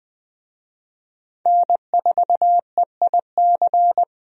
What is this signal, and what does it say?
Morse code, heard as a keyed tone, about 21 wpm, 700 Hz. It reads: N4EIC
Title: USA Call Signs - 5 Characters - 20wpm